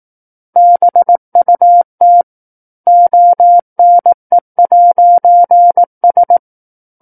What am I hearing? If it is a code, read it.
Morse code, heard as a keyed tone, about 18 wpm, 700 Hz. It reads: BUT ONE'S
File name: 18 WPM